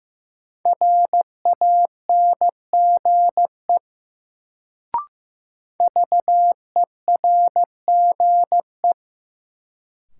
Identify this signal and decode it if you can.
Morse code, heard as a keyed tone, about 15 wpm, 700 Hz. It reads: RANGE VERGE
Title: Sets of 4 Words - New Word Drill - 15wpm